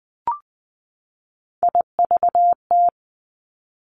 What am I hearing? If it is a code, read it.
Morse code, heard as a keyed tone, about 20 wpm, 700 Hz. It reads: IVT
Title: Three Challenging Letter-Number Combinations 20wpm